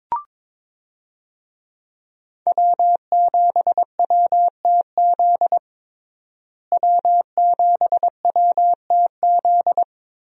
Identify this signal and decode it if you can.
Morse code, heard as a keyed tone, about 22 wpm, 700 Hz. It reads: W7WTZ W7WTZ
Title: USA Call Signs - 5 Characters - Repeated 3x - 22wpm